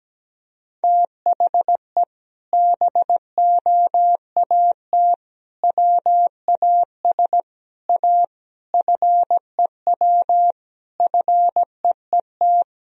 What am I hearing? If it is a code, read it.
Morse code, heard as a keyed tone, about 17 wpm, 700 Hz. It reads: THE BOAT WAS A FEW FEET
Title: Sentences from Top 1000 Words - Review Part 3 - 17wpm